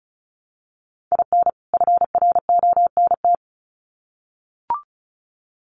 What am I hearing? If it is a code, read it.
Morse code, heard as a keyed tone, about 34 wpm, 700 Hz. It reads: IN FRONT
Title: Sets of 2 Words 35wpm